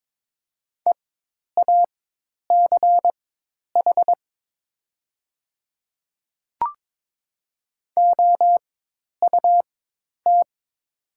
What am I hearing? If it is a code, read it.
Morse code, heard as a keyed tone, about 22 wpm, spaced as for 10 wpm, 700 Hz. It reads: EACH OUT
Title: 4x Character Spacing - Top 100 Words - 22wpm